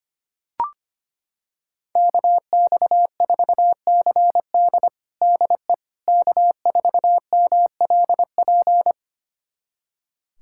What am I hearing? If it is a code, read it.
Morse code, heard as a keyed tone, about 25 wpm, 700 Hz. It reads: KX4CD DE K4MLP
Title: USA Callsign DE Callsign - 5 Characters - 25wpm